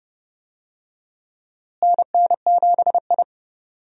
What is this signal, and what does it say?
Morse code, heard as a keyed tone, about 30 wpm, 700 Hz. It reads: NN7I
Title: USA Call Signs - 4 Characters - Repeated 2x - 30wpm